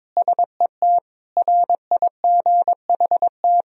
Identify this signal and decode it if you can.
Morse code, heard as a keyed tone, about 22 wpm, 700 Hz. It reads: SET RIGHT
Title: Sentences from Top 1000 Words - Review Part 1 - 22wpm